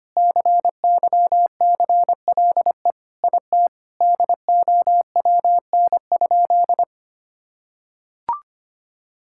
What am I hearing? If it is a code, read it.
Morse code, heard as a keyed tone, about 25 wpm, 700 Hz. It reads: CYCLE IT DOWN?
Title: Sentences from Top 1000 Words - Review Part 6 - 25wpm